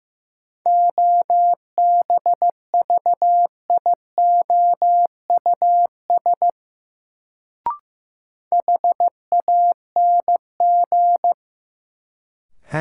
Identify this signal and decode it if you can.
Morse code, heard as a keyed tone, about 15 wpm, 700 Hz. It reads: OBVIOUS HANG
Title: Sentences from Top 1600 Words - New Word Drill - 15wpm